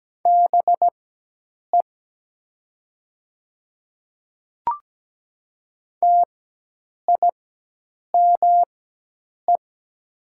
Morse code, heard as a keyed tone, about 17 wpm, spaced as for 8 wpm, 700 Hz. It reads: BE TIME
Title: 4x Character Spacing - Top 100 Words - 17wpm